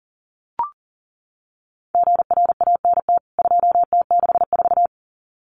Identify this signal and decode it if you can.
Morse code, heard as a keyed tone, about 40 wpm, 700 Hz. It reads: GRANT 2T64
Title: CWT Exchange 40wpm